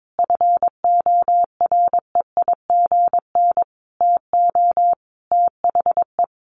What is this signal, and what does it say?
Morse code, heard as a keyed tone, about 22 wpm, 700 Hz. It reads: FOREIGN TO THE
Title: Sentences from Top 1400 Words 22wpm